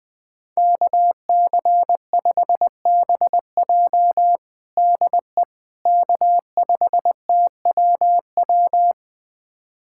Morse code, heard as a keyed tone, about 20 wpm, 700 Hz. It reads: KC5BJ DE K5TWW
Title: USA Callsign DE Callsign - 5 Characters - 20wpm